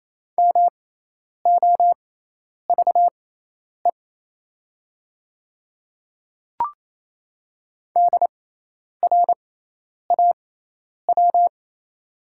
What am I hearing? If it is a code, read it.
Morse code, heard as a keyed tone, about 28 wpm, spaced as for 10 wpm, 690 Hz. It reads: MOVE DRAW
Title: 6x Character Spacing - Top 300 Words - 28wpm